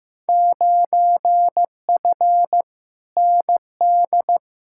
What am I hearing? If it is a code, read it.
Morse code, heard as a keyed tone, about 15 wpm, 700 Hz. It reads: 9F ND
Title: Field Day Exchanges Encore - 15wpm